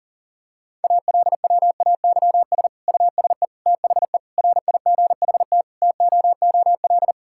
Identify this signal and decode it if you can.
Morse code, heard as a keyed tone, about 40 wpm, 700 Hz. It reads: ALWAYS USE THE RIGHT TOOL
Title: Sentences from Top 1000 Words - Review Part 6 - 40wpm